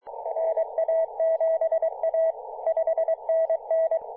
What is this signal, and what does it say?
Morse code, heard as a keyed tone, about 22 wpm, 650 Hz, with radio noise. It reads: NA7A 5NN
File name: vk9-m0vfc-14-013mhz-cw